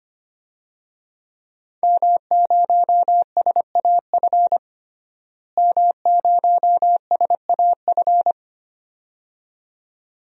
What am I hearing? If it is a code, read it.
Morse code, heard as a keyed tone, about 25 wpm, 700 Hz. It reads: M0SAF M0SAF
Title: DXCC-100 Call Signs - 5 Characters - Repeated 2x - 25wpm